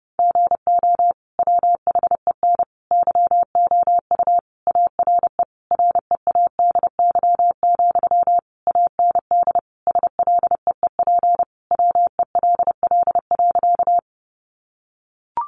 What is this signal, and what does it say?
Morse code, heard as a keyed tone, about 30 wpm, 700 Hz. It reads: GO WHEN YOU ARE READY, AND SLEEP WELL.
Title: Sentences from Top 500 Words 30wpm